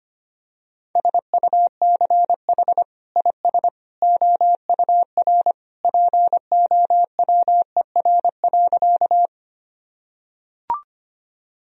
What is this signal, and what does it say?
Morse code, heard as a keyed tone, about 25 wpm, 700 Hz. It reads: SUCH IS OUR POWER.
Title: Sentences from Top 1000 Words - Review Part 2 - 25wpm